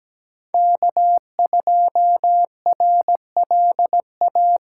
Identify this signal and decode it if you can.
Morse code, heard as a keyed tone, about 17 wpm, 700 Hz. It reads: K2RLA
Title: Code Groups - 5 Letter:Numbers - 17wpm